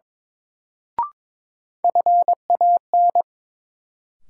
Morse code, heard as a keyed tone, about 22 wpm, 700 Hz. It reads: FAN
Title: Sentences from Top 1300 Words - New Word Drill Alt - 22wpm